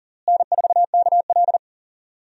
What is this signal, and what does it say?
Morse code, heard as a keyed tone, about 40 wpm, 700 Hz. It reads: N4KL
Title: USA Call Signs - 4 Characters - Repeated 2x - 40wpm